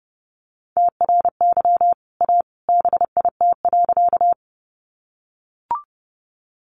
Morse code, heard as a keyed tone, about 30 wpm, 700 Hz. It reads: TRY A BIT.
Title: Sentences from Top 900 Words 30wpm